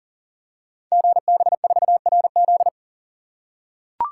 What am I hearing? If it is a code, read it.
Morse code, heard as a keyed tone, about 42 wpm, 690 Hz. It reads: GB4RZ
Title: Code Groups - 5 Letter:Numbers - 40wpm